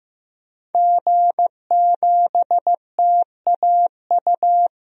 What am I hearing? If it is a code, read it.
Morse code, heard as a keyed tone, about 16 wpm, 700 Hz. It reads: G7TAU
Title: DXCC-100 Call Signs - 5 Characters - Repeated 2x - 15wpm